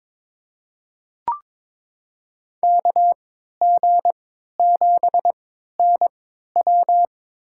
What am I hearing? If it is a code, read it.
Morse code, heard as a keyed tone, about 22 wpm, spaced as for 13 wpm, 700 Hz. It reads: KG7NW
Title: USA Call Signs - 5 Characters - 3x Character Spacing - 22wpm